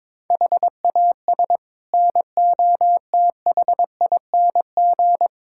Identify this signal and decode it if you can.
Morse code, heard as a keyed tone, about 22 wpm, 700 Hz. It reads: HAS NOTHING
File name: practice / 22wpm / Build A Phrase - 5-Word Phrases - 22wpm